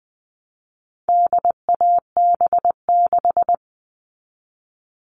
Morse code, heard as a keyed tone, about 20 wpm, 700 Hz. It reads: DAB6
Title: All Cut Numbers - 4-Digit 20wpm